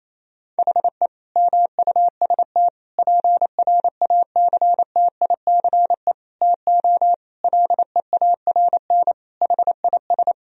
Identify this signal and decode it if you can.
Morse code, heard as a keyed tone, about 28 wpm, 700 Hz. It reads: HE MUST PRACTICE TO LEARN HIS
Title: Sentences from Top 1000 Words - Review Part 4 - 28wpm